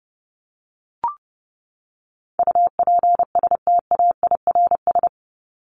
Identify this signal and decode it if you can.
Morse code, heard as a keyed tone, about 30 wpm, 700 Hz. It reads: UPSTAIRS
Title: Top 1751-2000 Words 30wpm